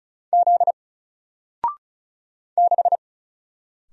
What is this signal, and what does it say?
Morse code, heard as a keyed tone, about 36 wpm, 700 Hz. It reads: Z 6
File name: Single-Letter-Number-35wpm